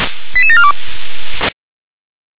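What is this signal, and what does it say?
Five-tone selective call (ZVEI), heard as quick runs of five tones, tones of about 70 ms. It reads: 88742
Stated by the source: Five-tone ZVEI selective call 88747 received by SDR.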